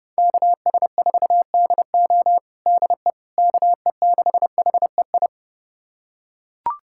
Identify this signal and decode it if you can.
Morse code, heard as a keyed tone, about 30 wpm, 700 Hz. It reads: KS4DO DE KE6HEI
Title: USA Callsign DE Callsign 30wpm